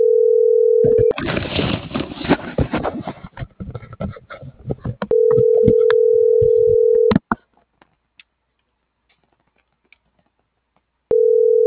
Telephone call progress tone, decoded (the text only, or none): ringback tone